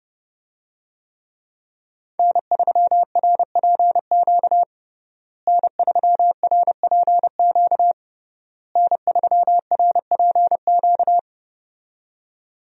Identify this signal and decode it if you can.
Morse code, heard as a keyed tone, about 32 wpm, 700 Hz. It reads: N3RPQ N3RPQ N3RPQ
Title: USA Call Signs - 5 Characters - Repeated 3x - 30wpm